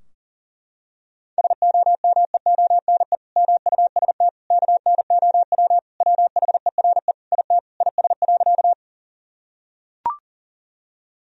Morse code, heard as a keyed tone, about 37 wpm, 700 Hz. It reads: SOMEONE MUST KNOW WHERE IT IS.
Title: Sentences from Top 1000 Words - Review Part 4 - 40wpm